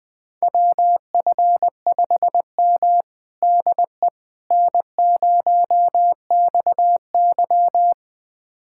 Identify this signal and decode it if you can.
Morse code, heard as a keyed tone, about 20 wpm, 700 Hz. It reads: WF5M DE N0XY
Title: USA Callsign DE Callsign - 4 Characters - 20wpm